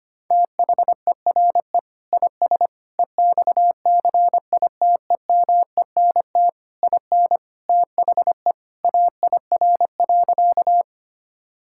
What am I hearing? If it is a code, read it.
Morse code, heard as a keyed tone, about 25 wpm, 700 Hz. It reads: THERE IS EXCITEMENT IN THE AIR.
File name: practice / 25wpm / Sentences with 3-Word Phrases - 25wpm